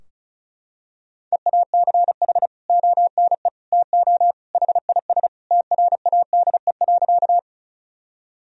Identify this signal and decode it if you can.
Morse code, heard as a keyed tone, about 36 wpm, 700 Hz. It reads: EACH ONE TO HIS TRADE.
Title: Sentences from Top 1000 Words - Review Part 4 - 35wpm